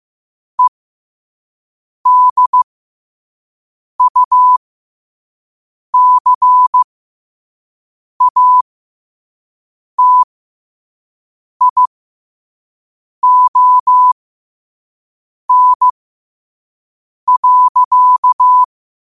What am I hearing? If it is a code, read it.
Morse code, heard as a keyed tone, about 15 wpm, spaced as for 5 wpm, 1000 Hz. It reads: EDUCATION.
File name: qotdfiles / 5wpm